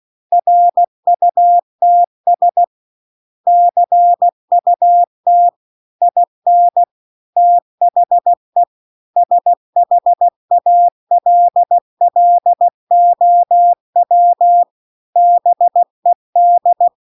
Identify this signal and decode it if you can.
Morse code, heard as a keyed tone, about 16 wpm, 700 Hz. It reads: RUTS CUT IN THE SHALLOW BED
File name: CW-The-Big-Twoheaded-River-2-16WPM